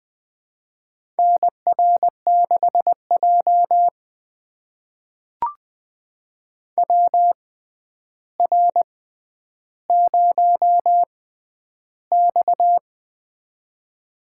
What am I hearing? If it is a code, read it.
Morse code, heard as a keyed tone, about 20 wpm, spaced as for 10 wpm, 700 Hz. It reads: NR6J WR0X
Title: USA Call Signs - 4 Characters - 6x Character Spacing - 20wpm